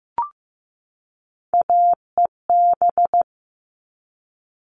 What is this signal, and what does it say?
Morse code, heard as a keyed tone, about 15 wpm, 700 Hz. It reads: AEB
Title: All Cut Numbers - 3-Digit 15wpm